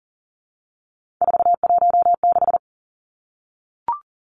Morse code, heard as a keyed tone, about 40 wpm, 700 Hz. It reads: 416
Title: Numbers - 3-Digit 40wpm